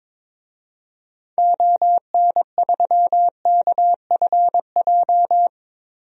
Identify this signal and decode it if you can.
Morse code, heard as a keyed tone, about 22 wpm, 700 Hz. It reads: ON3KFJ
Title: DXCC-100 Call Signs - 6 Characters - Repeated 3x - 22wpm